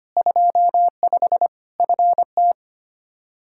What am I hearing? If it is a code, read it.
Morse code, heard as a keyed tone, about 25 wpm, 700 Hz. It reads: 25 FT
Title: Practice QSO - Exchange 2 - Info - 25wpm